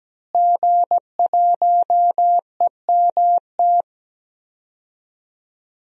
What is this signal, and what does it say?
Morse code, heard as a keyed tone, about 17 wpm, 700 Hz. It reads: G1EMT
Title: DXCC-100 Call Signs - 5 Characters - Repeated 2x - 17wpm